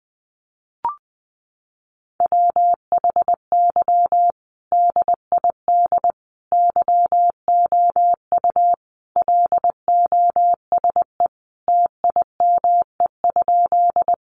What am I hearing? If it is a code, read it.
Morse code, heard as a keyed tone, about 20 wpm, 700 Hz. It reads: WHY DID YOU LOSE TIME?
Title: Sentences from Top 1100 Words 20wpm